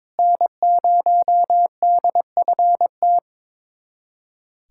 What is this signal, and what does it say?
Morse code, heard as a keyed tone, about 21 wpm, 700 Hz. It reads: N0DFT
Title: USA Callsign DE Callsign - 5 Characters - 22wpm